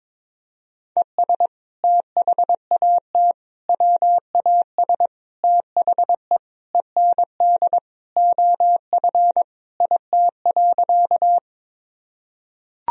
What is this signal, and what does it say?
Morse code, heard as a keyed tone, about 22 wpm, 700 Hz. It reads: ES THAT WAS THE END OF IT.
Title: Sentences from Top 200 Words and Abbreviations 22wpm